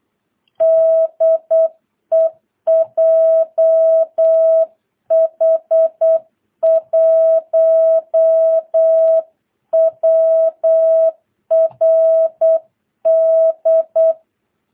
Morse code, heard as a keyed tone, about 8 wpm, 660 Hz. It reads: DE JH1WRD
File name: jh8cbh / 2022cw